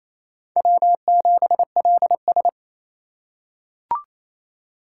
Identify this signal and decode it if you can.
Morse code, heard as a keyed tone, about 28 wpm, 700 Hz. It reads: W7LS
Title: USA Call Signs - 4 Characters - 2x Character Spacing - 28wpm